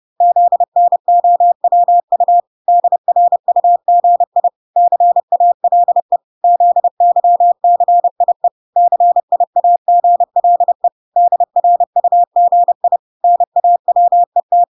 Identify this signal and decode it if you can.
Morse code, heard as a keyed tone, about 30 wpm, 700 Hz. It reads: ZNOWU DRUGI CALE ZYCIE CIAGLE DRUGI NAWET